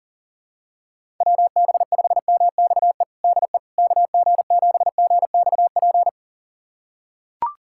Morse code, heard as a keyed tone, about 40 wpm, 700 Hz. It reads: WB5MXE DE KG7GXP
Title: USA Callsign DE Callsign - 6 Characters - 40wpm